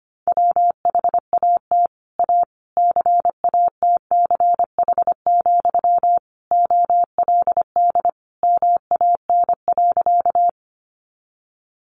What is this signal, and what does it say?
Morse code, heard as a keyed tone, about 25 wpm, 700 Hz. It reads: WHAT A CATCH, OLD MAN.
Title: Sentences from Top 1500 Words 25wpm